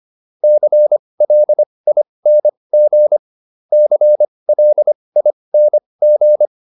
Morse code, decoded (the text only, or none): CLING CLING